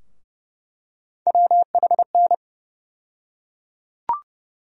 Morse code, heard as a keyed tone, about 31 wpm, 700 Hz. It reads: WHN
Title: USA Call Sign Postfixes - 3 Letters 30wpm